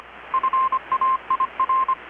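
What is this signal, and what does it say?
Morse code, heard as a keyed tone, about 25 wpm, 1050 Hz. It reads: FAIR
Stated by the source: Morse from KPH's Night of Nights XX maritime radiotelegraphy event, received in Chiapas, Mexico about 3,800 km away on a Kenwood R-600 with a grounded 10 m long-wire antenna.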